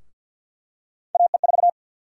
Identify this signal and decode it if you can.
Morse code, heard as a keyed tone, about 47 wpm, 700 Hz. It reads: AE4
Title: USA Call Sign Prefixes - 50wpm